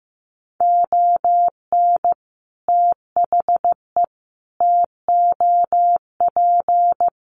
Morse code, heard as a keyed tone, about 15 wpm, 700 Hz. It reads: ON THE TOP
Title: Sets of 5 Words Encore 15wpm